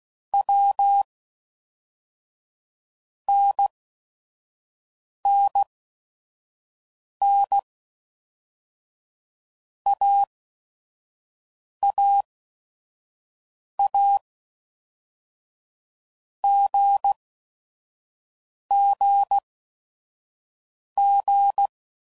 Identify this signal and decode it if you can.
Morse code, heard as a keyed tone, about 16 wpm, 800 Hz. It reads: W N N N A A A G G G
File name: Basic Letter Groups By Blocks / Groups@5 / AWUNDG@5 f15